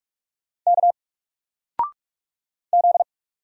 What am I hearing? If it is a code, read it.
Morse code, heard as a keyed tone, about 45 wpm, 700 Hz. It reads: K Z
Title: Single Letter-Number 45wpm